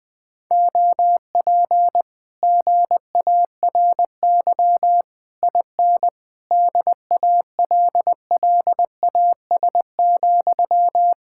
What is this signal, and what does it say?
Morse code, heard as a keyed tone, about 20 wpm, 700 Hz. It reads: OP GARY IN DALLAS,
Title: Speed-Racing Common QSO Phrases - 20wpm